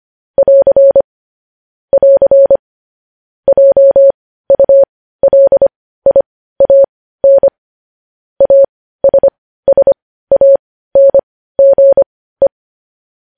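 Morse code, heard as a keyed tone, about 25 wpm, spaced as for 15 wpm, 550 Hz. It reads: <AR> <AR> JULIAN ASSANGE
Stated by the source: Morse code news headlines practice at 15, preceded by vvv.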